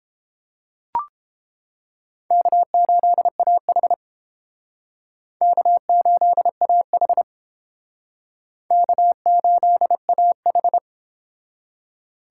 Morse code, heard as a keyed tone, about 31 wpm, 700 Hz. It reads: K8AH K8AH K8AH